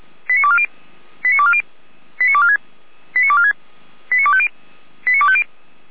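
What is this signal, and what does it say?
Five-tone selective call (ZVEI), heard as quick runs of five tones, tones of about 70 ms. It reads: 89240 89240 89246 89246 89250 89250
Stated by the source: ZVEI five-tone selective call received on SDR, Feuerwehr Orlamünde audio archive.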